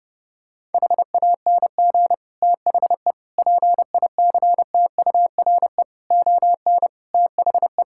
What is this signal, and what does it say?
Morse code, heard as a keyed tone, about 30 wpm, 700 Hz. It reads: HANG THE PICTURE ON THE
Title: Sentences from Top 1600 Words 30wpm